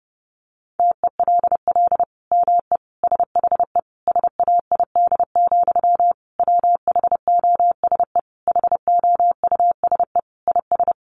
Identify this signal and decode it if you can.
Morse code, heard as a keyed tone, about 30 wpm, 700 Hz. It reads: TELL ME SHE SAID, WHOSE HOUSE IS
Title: Sentences from Top 1200 Words 30wpm